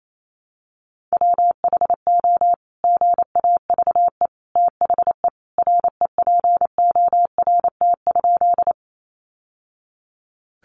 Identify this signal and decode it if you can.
Morse code, heard as a keyed tone, about 28 wpm, 700 Hz. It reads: WHO GAVE THE REPORT?
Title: Sentences from Top 900 Words 28wpm